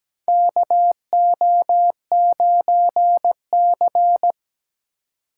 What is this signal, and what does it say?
Morse code, heard as a keyed tone, about 17 wpm, 700 Hz. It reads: KO9C
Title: USA Call Signs - 4 Characters - 8x Character Spacing - 17wpm